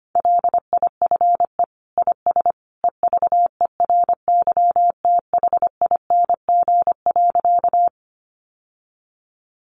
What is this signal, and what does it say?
Morse code, heard as a keyed tone, about 25 wpm, 700 Hz. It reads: LIFE IS EVERYTHING.
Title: Sentences from Top 700 Words 25wpm